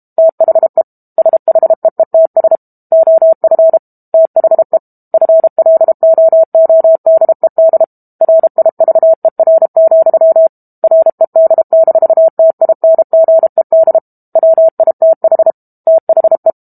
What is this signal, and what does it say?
Morse code, heard as a keyed tone, about 32 wpm, 650 Hz. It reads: THE SHEETS OF THE FLOODED RIVER, RED-TINGED WITH THE
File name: Book Two / 06-Chapter 6 - 32 WPM